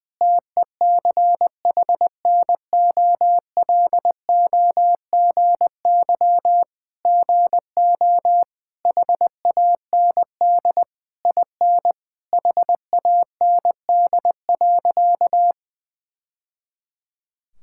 Morse code, heard as a keyed tone, about 20 wpm, 700 Hz. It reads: TECHNOLOGY GO HAND IN HAND.
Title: Sentences from Top 1000 Words - Review Part 5 - 20wpm